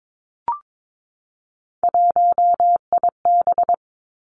Morse code, heard as a keyed tone, about 22 wpm, 700 Hz. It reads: 1IB
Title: Three Challenging Letter-Number Combinations 22wpm